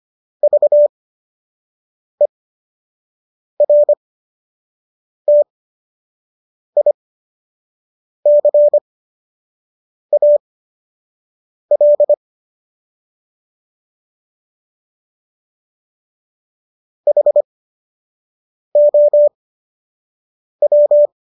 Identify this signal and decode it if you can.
Morse code, heard as a keyed tone, about 25 wpm, spaced as for 6 wpm, 590 Hz. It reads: VERTICAL HOW